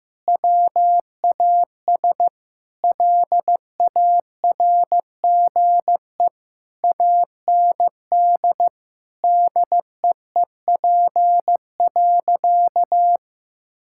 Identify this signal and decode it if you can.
Morse code, heard as a keyed tone, about 15 wpm, 700 Hz. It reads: WAS LARGE AND DEEP.
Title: Sentences from Top 1600 Words 15wpm